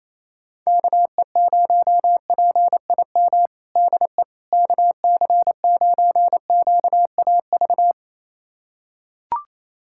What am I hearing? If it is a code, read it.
Morse code, heard as a keyed tone, about 28 wpm, 700 Hz. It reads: KE0PIM DE KC9QAV
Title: USA Callsign DE Callsign - 6 Characters - 28wpm